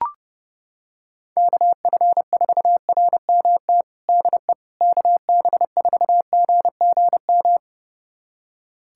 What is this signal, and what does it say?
Morse code, heard as a keyed tone, about 30 wpm, 700 Hz. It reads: KF4RMT DE KB4GGM
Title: USA Callsign DE Callsign - 6 Characters - 30wpm